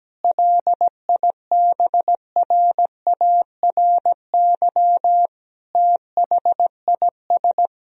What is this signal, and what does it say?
Morse code, heard as a keyed tone, about 17 wpm, 700 Hz. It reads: LIBRARY THIS
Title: Sentences from Top 1000 Words - Review Part 4 - 17wpm